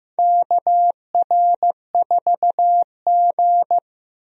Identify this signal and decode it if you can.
Morse code, heard as a keyed tone, about 15 wpm, 700 Hz. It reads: KR4G
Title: USA Call Signs - 4 Characters - Repeated 3x - 15wpm